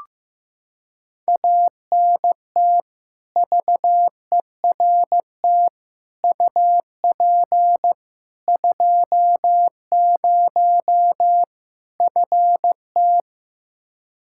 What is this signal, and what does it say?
Morse code, heard as a keyed tone, about 15 wpm, 700 Hz. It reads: ANT VERT UP 20 FT
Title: Practice QSO - Exchange 2 - Info - 15wpm